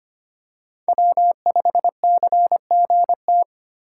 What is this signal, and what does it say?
Morse code, heard as a keyed tone, about 25 wpm, 700 Hz. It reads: W5CGT
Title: USA Call Signs - 5 Characters - 3x Character Spacing - 25wpm